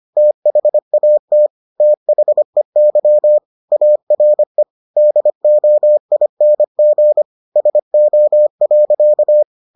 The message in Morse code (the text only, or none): THAT THEY ARE DOING SO.